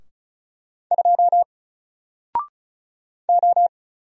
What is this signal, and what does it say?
Morse code, heard as a keyed tone, about 36 wpm, 700 Hz. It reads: 2 O
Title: Single Character (Letter:Number:Punctuation:Prosign) Alt 35wpm